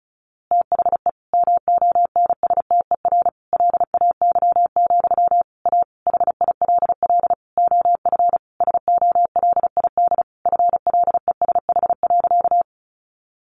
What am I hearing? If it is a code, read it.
Morse code, heard as a keyed tone, about 34 wpm, 700 Hz. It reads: THE MONSTER LAY, A HILL OF SOLID FLESH.
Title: A Sound Of Thunder 35wpm